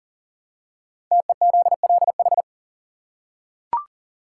Morse code, heard as a keyed tone, about 38 wpm, 690 Hz. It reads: TEZLH
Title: Code Groups - 5 Letters - 40wpm